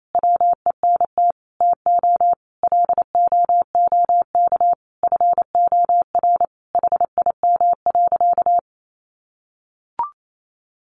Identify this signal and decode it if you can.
Morse code, heard as a keyed tone, about 28 wpm, 700 Hz. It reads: WENT TO LOOK FOR HIM.
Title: Sentences from Top 1100 Words 28wpm